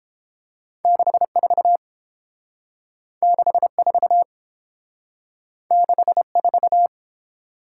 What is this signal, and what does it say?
Morse code, heard as a keyed tone, about 29 wpm, 700 Hz. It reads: 64 64 64